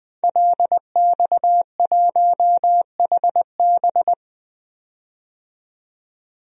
Morse code, heard as a keyed tone, about 20 wpm, 700 Hz. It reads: LX1HB
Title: DXCC-100 Call Signs - 5 Characters - 20wpm